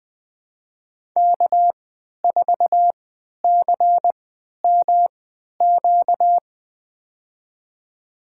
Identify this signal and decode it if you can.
Morse code, heard as a keyed tone, about 20 wpm, spaced as for 11 wpm, 700 Hz. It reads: K4CMQ
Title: USA Call Signs - 3x Character Spacing - 20wpm